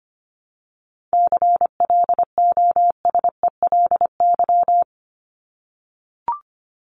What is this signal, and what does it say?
Morse code, heard as a keyed tone, about 25 wpm, 700 Hz. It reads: CLOSELY
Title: Top 1251-1500 Words 25wpm